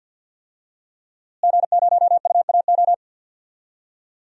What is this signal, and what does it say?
Morse code, heard as a keyed tone, about 47 wpm, 700 Hz. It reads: G0UAO
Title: DXCC-100 Call Signs - 5 Characters - Repeated 2x - 50wpm